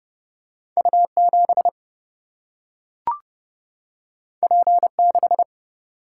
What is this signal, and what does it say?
Morse code, heard as a keyed tone, about 30 wpm, 700 Hz. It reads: U7 P6
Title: Single-Letter Single-Number 30wpm